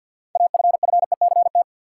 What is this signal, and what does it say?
Morse code, heard as a keyed tone, about 50 wpm, 700 Hz. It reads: AFFECT
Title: Sentences from Top 900 Words - New Word Drill - 50wpm